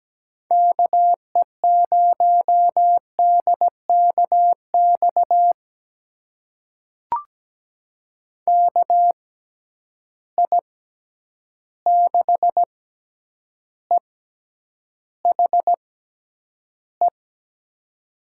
Morse code, heard as a keyed tone, about 17 wpm, spaced as for 9 wpm, 700 Hz. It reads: KE0DKX KI6EHE
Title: USA Call Signs - 6 Characters - 6x Character Spacing - 17wpm